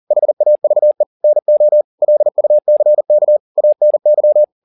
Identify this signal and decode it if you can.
Morse code, heard as a keyed tone, about 40 wpm, 600 Hz. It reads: HAVE NO LUCK ANY